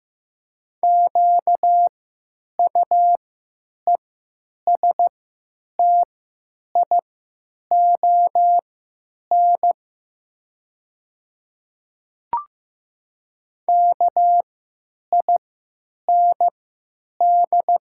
Morse code, heard as a keyed tone, about 15 wpm, spaced as for 9 wpm, 700 Hz. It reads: QUESTION KIND
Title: 3x Character Spacing - Top 500 Words - 15wpm